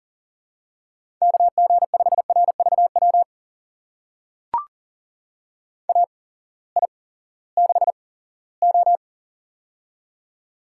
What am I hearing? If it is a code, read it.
Morse code, heard as a keyed tone, about 40 wpm, spaced as for 20 wpm, 700 Hz. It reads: KG5RVW AI6O
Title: USA Call Signs - 8x Character Spacing - 40wpm